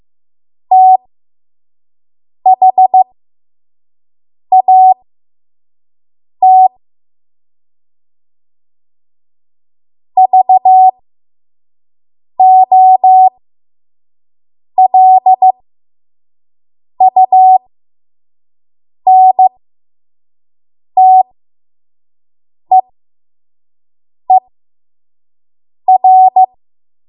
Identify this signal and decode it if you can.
Morse code, heard as a keyed tone, about 15 wpm, spaced as for 5 wpm, 750 Hz. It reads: THAT VOLUNTEER
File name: Morse / 05WPM